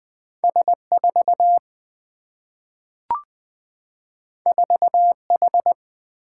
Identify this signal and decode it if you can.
Morse code, heard as a keyed tone, about 20 wpm, 700 Hz. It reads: S4 4H
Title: Single-Letter Single-Number 20wpm